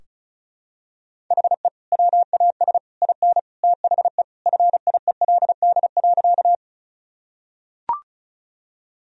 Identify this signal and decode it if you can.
Morse code, heard as a keyed tone, about 36 wpm, 700 Hz. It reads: HE WAS IN THE FIELD.
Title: Sentences from Top 1000 Words - Review Part 2 - 35wpm